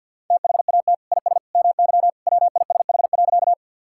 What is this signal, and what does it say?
Morse code, heard as a keyed tone, about 50 wpm, 700 Hz. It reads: THAT IS MY WISH.
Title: Sentences from Top 1600 Words 50wpm